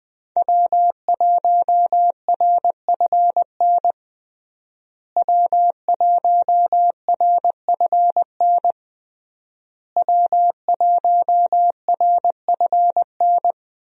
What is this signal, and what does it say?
Morse code, heard as a keyed tone, about 20 wpm, 700 Hz. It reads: W1RFN W1RFN W1RFN
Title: USA Call Signs - 5 Characters - Repeated 3x - 20wpm